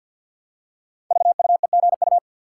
Morse code, heard as a keyed tone, about 52 wpm, 700 Hz. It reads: VUEGU
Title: Code Groups - 5 Letters - 50wpm